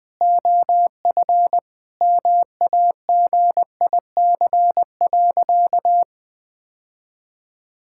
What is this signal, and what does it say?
Morse code, heard as a keyed tone, about 20 wpm, 700 Hz. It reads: OF MAGIC.
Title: Sentences with 3-Word Phrases - 20wpm